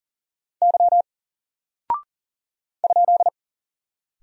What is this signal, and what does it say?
Morse code, heard as a keyed tone, about 39 wpm, 700 Hz. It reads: Y ?
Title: Single Character (Letter:Number:Punctuation:Prosign) Alt 40wpm